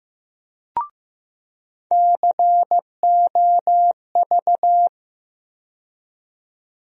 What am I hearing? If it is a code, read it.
Morse code, heard as a keyed tone, about 15 wpm, 700 Hz. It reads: COV
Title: USA Call Sign Postfixes - 3 Letters 15wpm